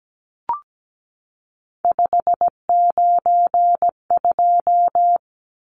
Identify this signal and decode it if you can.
Morse code, heard as a keyed tone, about 17 wpm, 700 Hz. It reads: 592
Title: Numbers - 3-Digit 17wpm